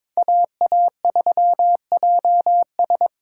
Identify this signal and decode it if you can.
Morse code, heard as a keyed tone, about 22 wpm, 700 Hz. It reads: AA3JS
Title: USA Call Signs - 5 Characters - 6x Character Spacing - 22wpm